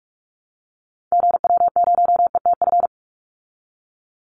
Morse code, heard as a keyed tone, about 44 wpm, 700 Hz. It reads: GW0ETF
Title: Call Signs Repeated 2x - 45wpm